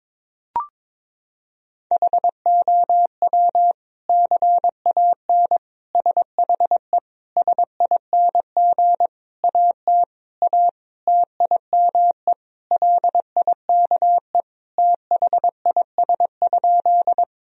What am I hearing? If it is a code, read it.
Morse code, heard as a keyed tone, about 22 wpm, 700 Hz. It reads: HOW CAN SHE SING AT A TIME LIKE THIS?
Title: Sentences from Top 1000 Words - Review Part 2 - 22wpm